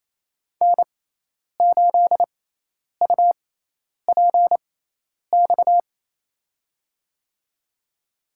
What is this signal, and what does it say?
Morse code, heard as a keyed tone, about 28 wpm, spaced as for 10 wpm, 700 Hz. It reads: N8UPX
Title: USA Call Signs - 6x Character Spacing - 28wpm